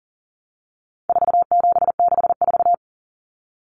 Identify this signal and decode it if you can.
Morse code, heard as a keyed tone, about 40 wpm, 700 Hz. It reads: 4764
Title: Numbers - 4-Digit 40wpm